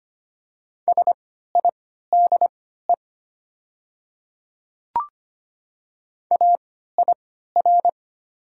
Morse code, heard as a keyed tone, about 25 wpm, spaced as for 14 wpm, 700 Hz. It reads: SIDE AIR
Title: 3x Character Spacing - Top 300 Words - 25wpm